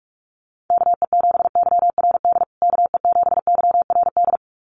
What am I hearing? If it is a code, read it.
Morse code, heard as a keyed tone, about 44 wpm, 690 Hz. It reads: KE7YRD KE7YRD
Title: USA Call Signs Repeated 2x 45wpm